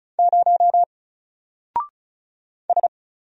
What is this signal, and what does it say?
Morse code, heard as a keyed tone, about 35 wpm, 700 Hz. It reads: 0 S
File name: Single-Letter-Number-35wpm